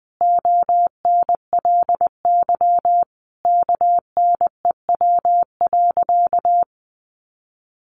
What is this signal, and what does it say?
Morse code, heard as a keyed tone, about 20 wpm, 700 Hz. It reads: ONLY KNEW.
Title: Sentences from Top 500 Words 20wpm